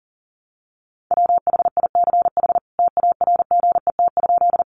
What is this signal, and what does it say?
Morse code, heard as a keyed tone, about 40 wpm, 700 Hz. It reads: WHICH TARGET?
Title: Sentences from Top 1200 Words 40wpm